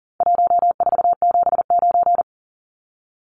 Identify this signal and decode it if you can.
Morse code, heard as a keyed tone, about 40 wpm, 700 Hz. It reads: 1479
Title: Numbers - Contest Style - 4-Digit 40wpm